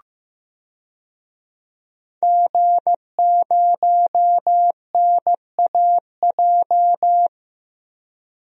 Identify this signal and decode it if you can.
Morse code, heard as a keyed tone, about 15 wpm, 700 Hz. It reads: G0NAJ
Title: DXCC-100 Call Signs - 5 Characters - Repeated 2x - 15wpm